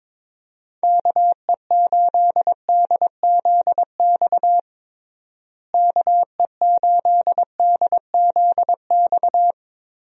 Morse code, heard as a keyed tone, about 22 wpm, 700 Hz. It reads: KE8DZX KE8DZX
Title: USA Call Signs - 6 Characters - Repeated 3x - 22wpm